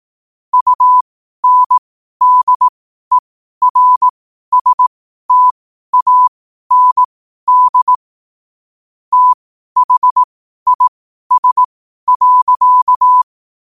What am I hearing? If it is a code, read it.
Morse code, heard as a keyed tone, about 18 wpm, spaced as for 12 wpm, 1000 Hz. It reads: UNDERSTAND THIS.
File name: qotdfiles / 12wpm